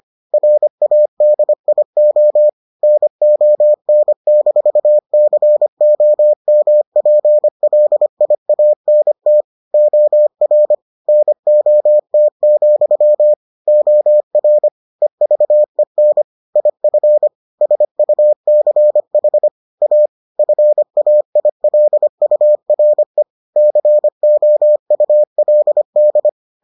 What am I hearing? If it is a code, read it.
Morse code, heard as a keyed tone, about 25 wpm, 600 Hz. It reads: RADIO NON-COMPLIANT OR NOT, OR EVEN IF SUCH A FAILURE COULD